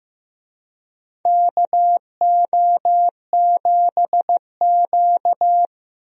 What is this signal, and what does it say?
Morse code, heard as a keyed tone, about 15 wpm, 700 Hz. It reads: KO7Q
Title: USA Call Signs - 4 Characters - Repeated 2x - 15wpm